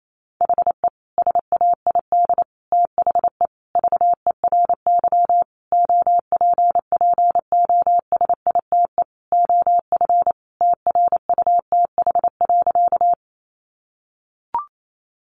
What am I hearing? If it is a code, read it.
Morse code, heard as a keyed tone, about 28 wpm, 700 Hz. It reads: HE SAID THE VERY OPPOSITE OF TRUTH.
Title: Sentences from Top 1300 Words 28wpm